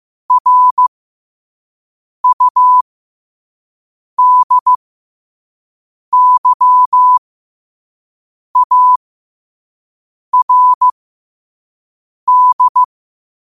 Morse code, heard as a keyed tone, about 15 wpm, spaced as for 5 wpm, 1000 Hz. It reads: RUDYARD
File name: qotdfiles / 5wpm